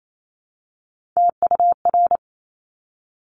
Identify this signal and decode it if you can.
Morse code, heard as a keyed tone, about 28 wpm, 700 Hz. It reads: TUR
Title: Top Three Letter Combinations in Words 28wpm